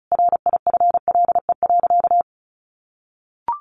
Morse code, heard as a keyed tone, about 34 wpm, 700 Hz. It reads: RIFLE.
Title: A Sound Of Thunder 35wpm